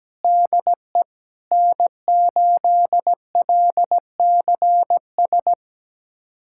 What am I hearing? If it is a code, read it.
Morse code, heard as a keyed tone, about 17 wpm, 700 Hz. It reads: DE N8LCS
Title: USA Callsign DE Callsign - 5 Characters - 17wpm